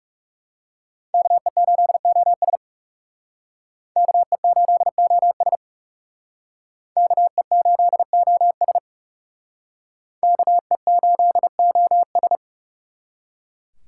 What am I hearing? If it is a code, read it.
Morse code, heard as a keyed tone, about 39 wpm, 700 Hz. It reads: KE8OS KE8OS KE8OS KE8OS